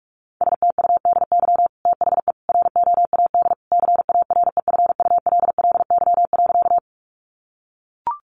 Morse code, heard as a keyed tone, about 44 wpm, 700 Hz. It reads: STUDY THE ROAD CAREFULLY.
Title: Sentences from Top 1300 Words 45wpm